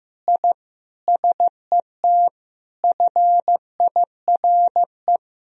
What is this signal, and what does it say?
Morse code, heard as a keyed tone, about 15 wpm, 700 Hz. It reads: I SET FIRE
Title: Sentences from Top 1000 Words - Review Part 1 - 15wpm